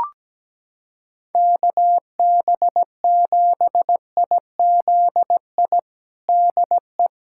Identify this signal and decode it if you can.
Morse code, heard as a keyed tone, about 17 wpm, 700 Hz. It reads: KB7IZI DE
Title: USA Callsign DE Callsign - 6 Characters - 17wpm